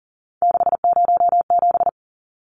Morse code, heard as a keyed tone, about 40 wpm, 700 Hz. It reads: 607
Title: Numbers - 3-Digit 40wpm